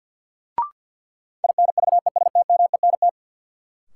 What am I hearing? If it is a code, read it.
Morse code, heard as a keyed tone, about 46 wpm, 700 Hz. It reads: INVESTMENT
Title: Sentences from Top 700 Words - New Word Drill - 50wpm